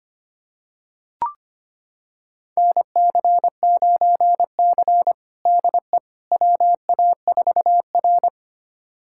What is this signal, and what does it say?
Morse code, heard as a keyed tone, about 25 wpm, 700 Hz. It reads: NC9C DE WA4R
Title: USA Callsign DE Callsign - 4 Characters - 25wpm